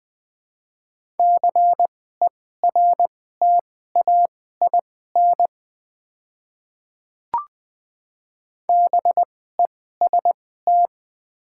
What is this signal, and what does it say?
Morse code, heard as a keyed tone, about 21 wpm, spaced as for 15 wpm, 700 Hz. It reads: CERTAIN BEST
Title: 2x Character Spacing - Top 500 Words - 20wpm